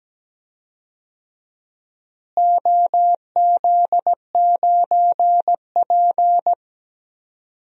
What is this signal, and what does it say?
Morse code, heard as a keyed tone, about 17 wpm, 700 Hz. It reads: OZ9P
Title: DXCC-100 Call Signs - 4 Characters - Repeated 3x - 17wpm